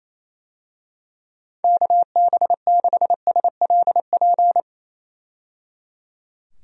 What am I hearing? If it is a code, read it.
Morse code, heard as a keyed tone, about 28 wpm, 700 Hz. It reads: KB6SLP
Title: USA Call Signs - 6 Characters - 28wpm